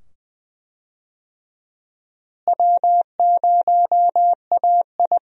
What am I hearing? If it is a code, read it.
Morse code, heard as a keyed tone, about 20 wpm, 700 Hz. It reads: W0AI
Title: USA Call Signs - 4 Characters - Repeated 2x - 20wpm